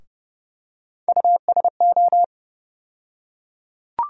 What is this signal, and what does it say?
Morse code, heard as a keyed tone, about 30 wpm, 700 Hz. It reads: USO
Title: USA Call Sign Postfixes - 3 Letters 30wpm